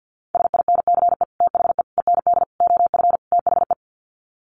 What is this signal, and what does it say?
Morse code, heard as a keyed tone, about 50 wpm, 700 Hz. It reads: SINCE THE END OF THE
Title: Sets of 5 Words Encore 50wpm